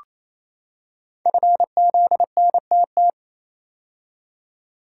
Morse code, heard as a keyed tone, about 29 wpm, 700 Hz. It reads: FZNTT
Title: Code Groups - 5 Letters - 28wpm